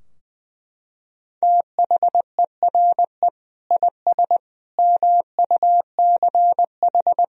Morse code, heard as a keyed tone, about 20 wpm, 700 Hz. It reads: THERE IS MUCH
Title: Sentences from Top 1000 Words - Review Part 4 - 20wpm